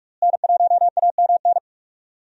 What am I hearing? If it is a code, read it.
Morse code, heard as a keyed tone, about 46 wpm, 700 Hz. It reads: N1AMN
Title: USA Call Signs - 5 Characters - Repeated 3x - 45wpm